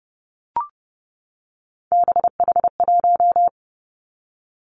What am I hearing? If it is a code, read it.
Morse code, heard as a keyed tone, about 31 wpm, 700 Hz. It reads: BH1
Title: Three Challenging Letter-Number Combinations 30wpm